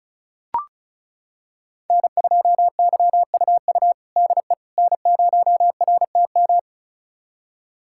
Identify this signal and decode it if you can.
Morse code, heard as a keyed tone, about 34 wpm, 700 Hz. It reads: N2YUU DE N0RTM
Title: USA Callsign DE Callsign - 5 Characters - 35wpm